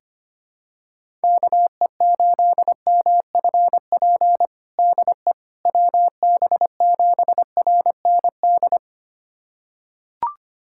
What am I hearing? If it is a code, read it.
Morse code, heard as a keyed tone, about 25 wpm, 700 Hz. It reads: KE8MFP DE WB7RND
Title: USA Callsign DE Callsign - 6 Characters - 25wpm